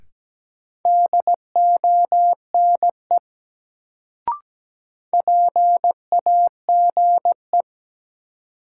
Morse code, heard as a keyed tone, about 17 wpm, 700 Hz. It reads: DONE PAGE
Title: Sets of 5 Words Encore - New Word Drill - 17wpm